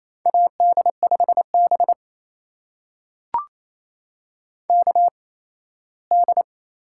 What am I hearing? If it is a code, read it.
Morse code, heard as a keyed tone, about 28 wpm, spaced as for 17 wpm, 700 Hz. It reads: AD5B KD
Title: USA Call Signs - 8x Character Spacing - 28wpm